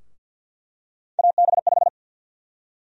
Morse code, heard as a keyed tone, about 50 wpm, 700 Hz. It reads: AB5
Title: USA Call Sign Prefixes - Repeated 2x - 50wpm